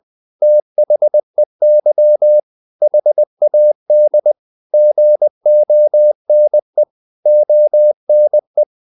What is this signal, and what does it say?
Morse code, heard as a keyed tone, about 20 wpm, 600 Hz. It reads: THEY HAD GONE ONE
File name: the old man and the sea / the old man and the sea part 1 20wpm